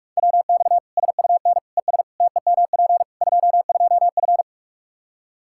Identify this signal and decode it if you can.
Morse code, heard as a keyed tone, about 46 wpm, 700 Hz. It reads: WX SUN ES TEMP 22F
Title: Practice QSO - Exchange 2 - Info - 45wpm